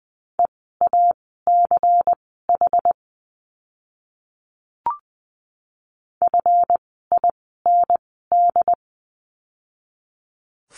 Morse code, heard as a keyed tone, about 20 wpm, spaced as for 15 wpm, 700 Hz. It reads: EACH FIND
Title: 192 Lesson 31f - Letter v - Words 20wpm